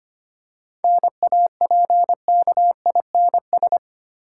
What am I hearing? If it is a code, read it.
Morse code, heard as a keyed tone, about 25 wpm, 690 Hz. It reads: NAPKINS
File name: practice / 25wpm / Easy Words with Hints - 25wpm